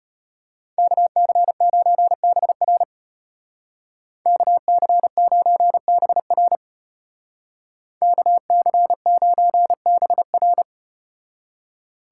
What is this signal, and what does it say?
Morse code, heard as a keyed tone, about 33 wpm, 700 Hz. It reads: KC9BR KC9BR KC9BR